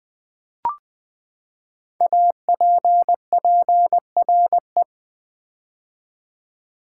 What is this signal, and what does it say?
Morse code, heard as a keyed tone, about 20 wpm, 700 Hz. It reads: APPRE
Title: Build A Word - 5 Letters At A Time - 12-Letter Words - 20wpm